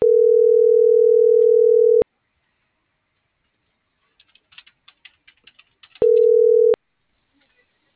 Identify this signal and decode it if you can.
Ringback tone (North America), 440 + 480 Hz.